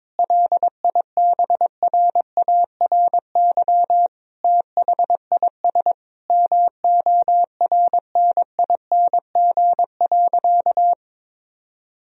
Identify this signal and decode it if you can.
Morse code, heard as a keyed tone, about 22 wpm, 700 Hz. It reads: LIBRARY THIS MORNING.
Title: Sentences from Top 1000 Words - Review Part 4 - 22wpm